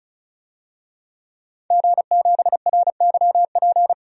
Morse code, heard as a keyed tone, about 34 wpm, 700 Hz. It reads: G7RYP
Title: DXCC-100 Call Signs - 5 Characters - 35wpm